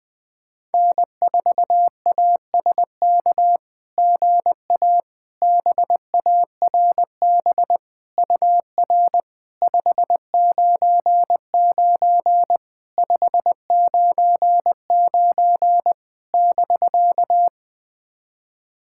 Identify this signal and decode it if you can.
Morse code, heard as a keyed tone, about 20 wpm, 700 Hz. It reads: N4ASK GA BARB UR 599 599 <BK>
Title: SOTA QSO Practice 20wpm